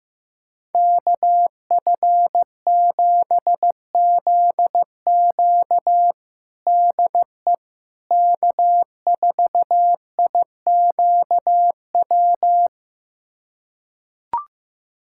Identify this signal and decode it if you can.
Morse code, heard as a keyed tone, about 15 wpm, 700 Hz. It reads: KF7ZQ DE K4IQW
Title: USA Callsign DE Callsign - 5 Characters - 15wpm